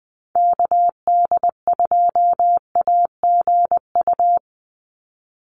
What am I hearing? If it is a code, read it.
Morse code, heard as a keyed tone, about 20 wpm, 700 Hz. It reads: KD2AGU
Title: USA Call Signs 20wpm